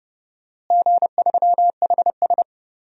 Morse code, heard as a keyed tone, about 30 wpm, 700 Hz. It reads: G3HS
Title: DXCC-100 Call Signs - 4 Characters - Repeated 3x - 30wpm